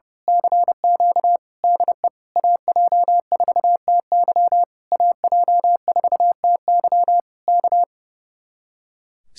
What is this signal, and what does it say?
Morse code, heard as a keyed tone, about 28 wpm, 700 Hz. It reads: CQ DE AJ4TY AJ4TY K
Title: Speed-Racing Common QSO Phrases - 30wpm